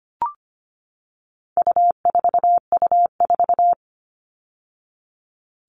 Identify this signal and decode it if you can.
Morse code, heard as a keyed tone, about 25 wpm, 700 Hz. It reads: U4U4
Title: All Cut Numbers - 4-Digit 25wpm